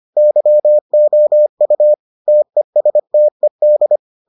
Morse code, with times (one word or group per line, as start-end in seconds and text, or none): 0.2-1.9 YOU
2.3-4.0 TESTED